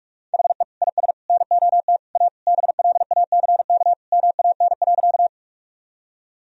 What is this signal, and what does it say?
Morse code, heard as a keyed tone, about 41 wpm, 700 Hz. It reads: HE IS NOT A BLACK MAN.
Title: Sentences from Top 1000 Words - Review Part 2 - 45wpm